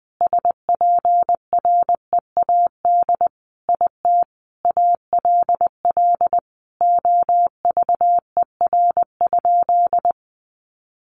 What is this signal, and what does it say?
Morse code, heard as a keyed tone, about 20 wpm, 700 Hz. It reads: SPREAD IT ALL OVER?
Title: Sentences from Top 1200 Words 20wpm